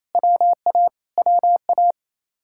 Morse code, heard as a keyed tone, about 26 wpm, 700 Hz. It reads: WA WA
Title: Speed-Racing Common QSO Phrases - 28wpm